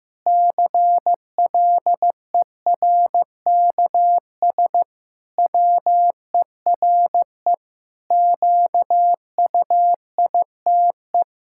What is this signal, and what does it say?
Morse code, heard as a keyed tone, about 16 wpm, 700 Hz. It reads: CLERKS WERE QUITE
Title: Pangram Challenge v2 - 15wpm